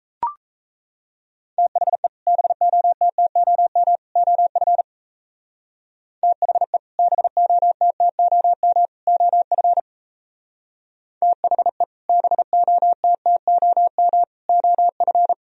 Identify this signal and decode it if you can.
Morse code, heard as a keyed tone, about 37 wpm, 700 Hz. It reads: THE BOTTOM OF THE BOTTOM OF THE BOTTOM OF